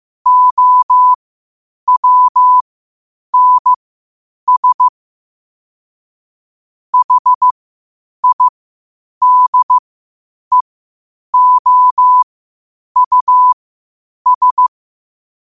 Morse code, heard as a keyed tone, about 15 wpm, spaced as for 8 wpm, 1000 Hz. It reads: OWNS HIDEOUS
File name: qotdfiles / 8wpm